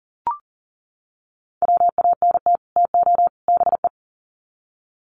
Morse code, heard as a keyed tone, about 37 wpm, 700 Hz. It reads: WANT TO BE
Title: Sets of 3 Words 40wpm